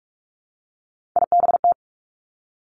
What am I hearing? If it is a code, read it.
Morse code, heard as a keyed tone, about 46 wpm, 700 Hz. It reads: IBT
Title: Three Challenging Letter-Number Combinations 45wpm